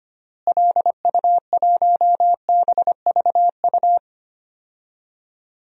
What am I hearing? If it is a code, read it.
Morse code, heard as a keyed tone, about 25 wpm, 700 Hz. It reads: LU1BVU
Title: DXCC-100 Call Signs - 6 Characters - Repeated 3x - 25wpm